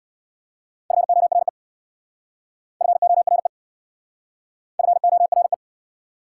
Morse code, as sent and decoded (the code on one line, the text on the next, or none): ..-. --- .-. / ..-. --- .-. / ..-. --- .-. .
FORE FORE FORE